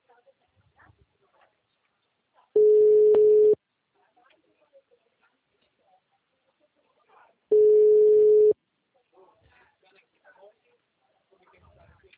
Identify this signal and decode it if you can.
Ringback tone (Europe, 425 Hz), with speech.